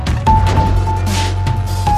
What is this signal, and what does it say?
There is no code.